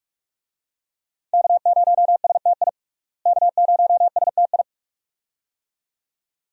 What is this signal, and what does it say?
Morse code, heard as a keyed tone, about 42 wpm, 700 Hz. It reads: K0STI K0STI
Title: USA Call Signs - 5 Characters - Repeated 2x - 45wpm